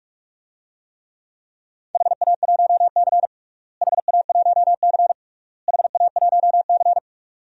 Morse code, heard as a keyed tone, about 46 wpm, 700 Hz. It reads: HA1C HA1C HA1C
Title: DXCC-100 Call Signs - 4 Characters - Repeated 3x - 45wpm